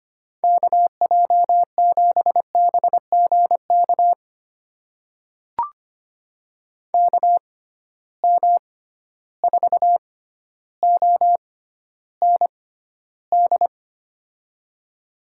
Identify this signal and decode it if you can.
Morse code, heard as a keyed tone, about 25 wpm, spaced as for 13 wpm, 700 Hz. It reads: KJ7BGK KM4OND
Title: USA Call Signs - 6 Characters - 6x Character Spacing - 25wpm